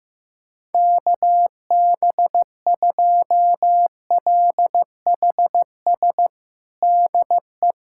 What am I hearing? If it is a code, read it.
Morse code, heard as a keyed tone, about 15 wpm, 700 Hz. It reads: KB2LHS DE
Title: USA Callsign DE Callsign 15wpm